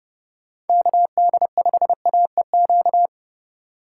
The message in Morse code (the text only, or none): KD5AEQ